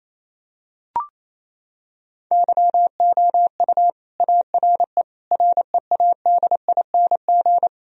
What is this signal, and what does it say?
Morse code, heard as a keyed tone, about 27 wpm, 700 Hz. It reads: YOU ARE READING
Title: Sentences from Top 1000 Words - Review Part 3 - 28wpm